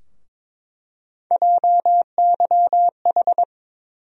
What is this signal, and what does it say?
Morse code, heard as a keyed tone, about 22 wpm, 700 Hz. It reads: JYH
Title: USA Call Sign Postfixes - 3 Letters 22wpm